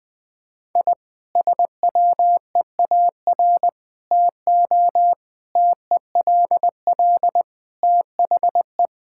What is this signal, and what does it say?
Morse code, heard as a keyed tone, about 20 wpm, 700 Hz. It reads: I SWEAR TO TELL THE
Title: Sentences with 9-Word Phrases - 20wpm